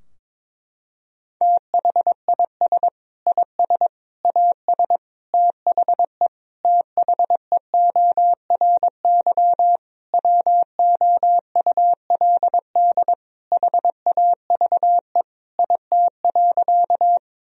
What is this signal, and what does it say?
Morse code, heard as a keyed tone, about 22 wpm, 700 Hz. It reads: THIS IS AS THE THEORY WOULD HAVE IT.
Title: Sentences from Top 1000 Words - Review Part 3 - 22wpm